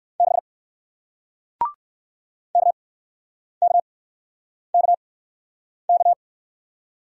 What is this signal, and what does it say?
Morse code, heard as a keyed tone, about 57 wpm, spaced as for 9 wpm, 700 Hz. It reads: 6 KKKK